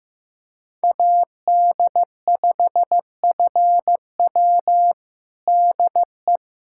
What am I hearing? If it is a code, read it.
Morse code, heard as a keyed tone, about 15 wpm, 700 Hz. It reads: AD5FW DE
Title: USA Callsign DE Callsign - 5 Characters - 15wpm